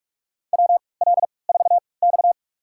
Morse code, heard as a keyed tone, about 47 wpm, spaced as for 26 wpm, 700 Hz. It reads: WL4X
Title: USA Call Signs - 4 Characters - 3x Character Spacing - 45wpm